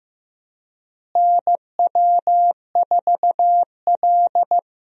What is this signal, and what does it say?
Morse code, heard as a keyed tone, about 15 wpm, 700 Hz. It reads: NW4L
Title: USA Call Signs - 4 Characters - Repeated 3x - 15wpm